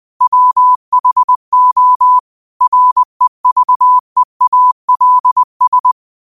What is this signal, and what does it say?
Morse code, heard as a keyed tone, about 20 wpm, 1000 Hz. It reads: WHO REVEALS